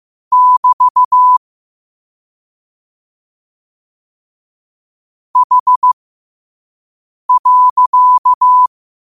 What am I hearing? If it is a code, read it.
Morse code, heard as a keyed tone, about 15 wpm, spaced as for 5 wpm, 1000 Hz. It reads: <BT> H.
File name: qotdfiles / 5wpm